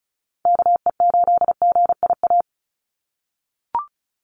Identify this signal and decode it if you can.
Morse code, heard as a keyed tone, about 34 wpm, 700 Hz. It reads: KE8GIA
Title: USA Call Signs 35wpm